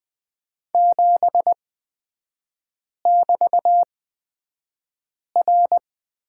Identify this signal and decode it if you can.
Morse code, heard as a keyed tone, about 20 wpm, spaced as for 5 wpm, 700 Hz. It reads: 7<BT>R
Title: 224 Lesson 36e - BT - Three Characters 20wpm